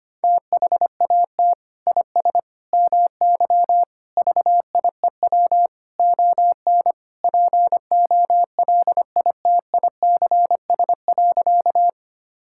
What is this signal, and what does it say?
Morse code, heard as a keyed tone, about 25 wpm, 700 Hz. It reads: THAT IS MY VIEW ON POLITICS.
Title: Sentences from Top 1000 Words - Review Part 6 - 25wpm